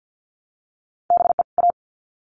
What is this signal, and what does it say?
Morse code, heard as a keyed tone, about 50 wpm, 700 Hz. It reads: BE A
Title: Sets of 2 Words 50wpm